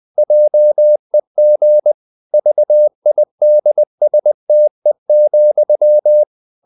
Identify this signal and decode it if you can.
Morse code, heard as a keyed tone, about 20 wpm, 600 Hz. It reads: JEG VIDSTE,
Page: Ferie 20 wpm